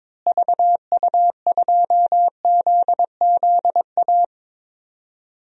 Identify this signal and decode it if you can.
Morse code, heard as a keyed tone, about 22 wpm, 700 Hz. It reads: VU2ZZA
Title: DXCC-100 Call Signs - 6 Characters - Repeated 3x - 22wpm